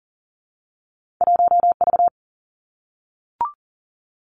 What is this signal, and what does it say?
Morse code, heard as a keyed tone, about 39 wpm, 700 Hz. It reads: 1V
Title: Two Challenging Letter-Number Combinations 40wpm